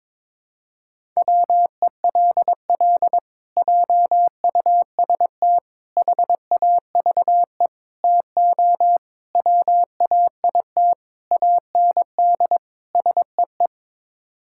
Morse code, heard as a keyed tone, about 22 wpm, 700 Hz. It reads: WELL JUST HAVE TO WAIT AND SEE
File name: practice / 22wpm / Sets of 7 Words - 22wpm